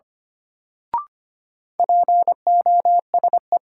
Morse code, heard as a keyed tone, about 25 wpm, 700 Hz. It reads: POSE
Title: Sentences from Top 1000 Words - Review Part 2 - New Word Drill - 25wpm